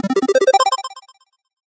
There is no code.